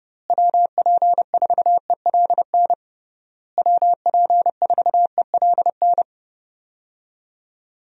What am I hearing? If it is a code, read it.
Morse code, heard as a keyed tone, about 30 wpm, 700 Hz. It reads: WP4ELN WP4ELN
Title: USA Call Signs - 6 Characters - Repeated 3x - 30wpm